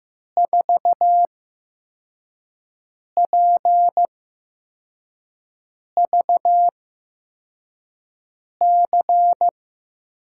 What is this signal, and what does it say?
Morse code, heard as a keyed tone, about 15 wpm, spaced as for 4 wpm, 700 Hz. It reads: 4PVC
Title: USA Call Signs - 8x Character Spacing - 15wpm